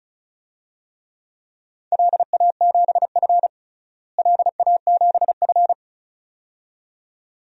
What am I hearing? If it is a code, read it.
Morse code, heard as a keyed tone, about 36 wpm, 700 Hz. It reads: LA7F LA7F
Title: DXCC-100 Call Signs - 4 Characters - Repeated 2x - 35wpm